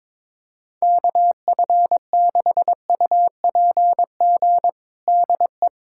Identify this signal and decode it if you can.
Morse code, heard as a keyed tone, about 21 wpm, 700 Hz. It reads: KF6UPG DE
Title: USA Callsign DE Callsign - 6 Characters - 22wpm